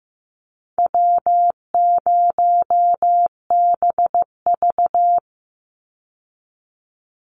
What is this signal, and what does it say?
Morse code, heard as a keyed tone, about 15 wpm, 700 Hz. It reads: W0BV
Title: Easy Call Signs Repeated 2x - 15wpm